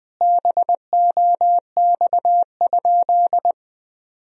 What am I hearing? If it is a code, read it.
Morse code, heard as a keyed tone, about 20 wpm, 700 Hz. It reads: BOX?
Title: Sentences from Top 1000 Words - Review Part 4 - 20wpm